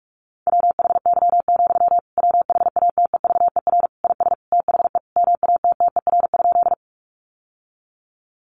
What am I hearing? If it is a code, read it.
Morse code, heard as a keyed tone, about 44 wpm, 700 Hz. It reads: WHY, WHATEVER IS THE MATTER?
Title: Sentences from Top 1100 Words 45wpm